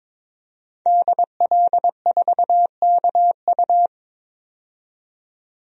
Morse code, heard as a keyed tone, about 22 wpm, 700 Hz. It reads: DL4KU
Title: DXCC-100 Call Signs - 5 Characters - 22wpm